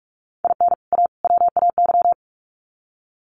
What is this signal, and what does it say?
Morse code, heard as a keyed tone, about 44 wpm, 700 Hz. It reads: IN A WAY
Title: Sets of 3 Words 45wpm